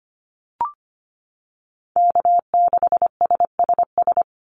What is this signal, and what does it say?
Morse code, heard as a keyed tone, about 25 wpm, 700 Hz. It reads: K6SSS
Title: Call Signs Repeated 3x - 25wpm